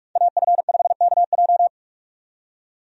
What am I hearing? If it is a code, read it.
Morse code, heard as a keyed tone, about 46 wpm, 700 Hz. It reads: AF5KJ
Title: USA Call Signs - 5 Characters - 45wpm